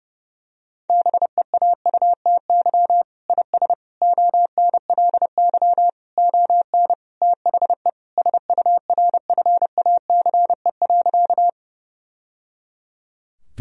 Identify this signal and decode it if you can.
Morse code, heard as a keyed tone, about 30 wpm, 700 Hz. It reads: BEAUTY IS ONLY ON THE SURFACE.
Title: Sentences from Top 1000 Words - Review Part 3 - 30wpm